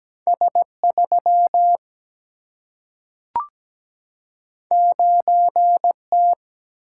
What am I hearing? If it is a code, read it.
Morse code, heard as a keyed tone, about 17 wpm, 700 Hz. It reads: S 3 9 T
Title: Single-Letter Single-Number 17wpm